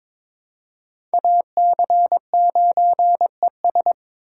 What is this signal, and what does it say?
Morse code, heard as a keyed tone, about 21 wpm, 700 Hz. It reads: AC9ES
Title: USA Call Signs - 3x Character Spacing - 22wpm